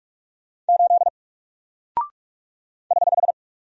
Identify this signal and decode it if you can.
Morse code, heard as a keyed tone, about 44 wpm, 700 Hz. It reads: 8 <HH>
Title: Single Character (Letter:Number:Punctuation:Prosign) 45wpm